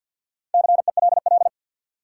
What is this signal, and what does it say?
Morse code, heard as a keyed tone, about 48 wpm, 700 Hz. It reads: CELL
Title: Sentences from Top 1100 Words - New Word Drill Alt - 50wpm